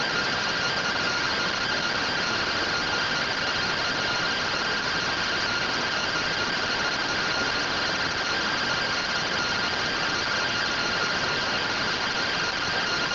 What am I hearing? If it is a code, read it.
Two-tone teleprinter signal, about 170 Hz shift.
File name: findings / MR AXIAL fr1 120.gen-29 rtty rx